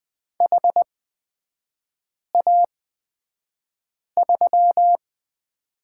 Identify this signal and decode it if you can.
Morse code, heard as a keyed tone, about 20 wpm, spaced as for 5 wpm, 690 Hz. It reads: HA3
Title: 236 Lesson 38e - Letter q - Three Characters 20wpm